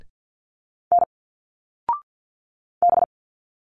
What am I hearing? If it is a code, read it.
Morse code, heard as a keyed tone, about 50 wpm, 700 Hz. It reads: N B
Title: Single Letters 50wpm